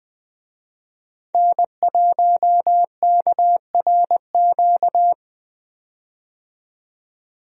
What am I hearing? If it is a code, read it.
Morse code, heard as a keyed tone, about 21 wpm, 700 Hz. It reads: N1KRQ
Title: USA Call Signs - 5 Characters - 20wpm